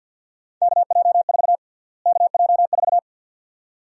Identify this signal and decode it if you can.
Morse code, heard as a keyed tone, about 47 wpm, 700 Hz. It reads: KJ4 KJ4
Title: USA Call Sign Prefixes - Repeated 2x - 50wpm